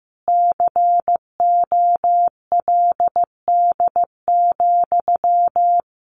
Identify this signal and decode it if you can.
Morse code, heard as a keyed tone, about 15 wpm, 700 Hz. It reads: COLD,
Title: Sentences from Top 500 Words 15wpm